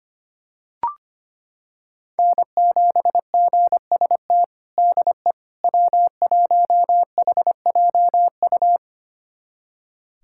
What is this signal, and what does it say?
Morse code, heard as a keyed tone, about 25 wpm, 700 Hz. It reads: N7GST DE W1HJU
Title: USA Callsign DE Callsign - 5 Characters - 25wpm